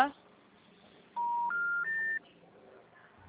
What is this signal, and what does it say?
Special information tone, with speech.